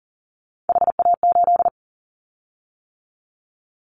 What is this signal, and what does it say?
Morse code, heard as a keyed tone, about 40 wpm, 700 Hz. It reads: HA8
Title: Callsign Prefixes for Top 50 DXCC - 40wpm